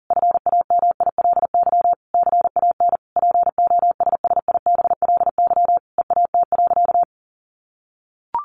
Morse code, heard as a keyed tone, about 40 wpm, 700 Hz. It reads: FAMILY CAN POSSIBLY EAT.
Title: Sentences from Top 1300 Words 40wpm